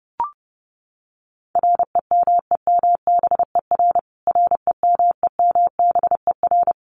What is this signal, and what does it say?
Morse code, heard as a keyed tone, about 30 wpm, 700 Hz. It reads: REMEMBER REMEMBER
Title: Top 500 Words Repeated 30wpm